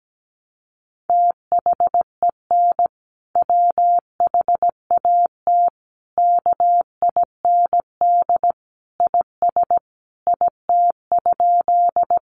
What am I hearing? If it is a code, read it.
Morse code, heard as a keyed tone, about 17 wpm, 700 Hz. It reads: THEN WHAT KIND IS IT?
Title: Sentences from Top 200 Words 17wpm